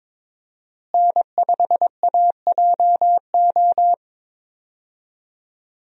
Morse code, heard as a keyed tone, about 21 wpm, 700 Hz. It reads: N5AJO
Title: USA Call Signs - 5 Characters - Repeated 3x - 22wpm